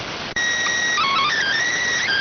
SSTV (slow-scan television), PD 290.